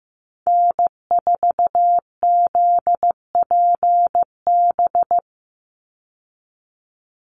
Morse code, heard as a keyed tone, about 15 wpm, 700 Hz. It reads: N4ZPB
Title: USA Call Signs 15wpm